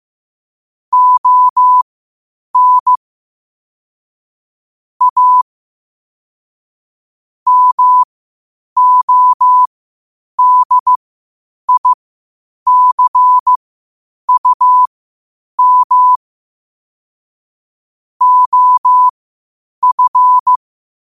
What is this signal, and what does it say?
Morse code, heard as a keyed tone, about 15 wpm, spaced as for 8 wpm, 1000 Hz. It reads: ON A MODICUM OF
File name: qotdfiles / 8wpm